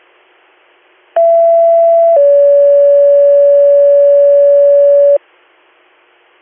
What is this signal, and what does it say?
Two-tone page, heard as a one-second tone, then a three-second tone, about 670 then 570 Hz. It is heard from 1.2 to 5.2 s.